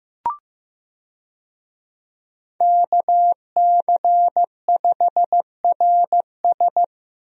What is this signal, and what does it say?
Morse code, heard as a keyed tone, about 15 wpm, 700 Hz. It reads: KC5RS
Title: USA Call Signs - 5 Characters - Repeated 2x - 15wpm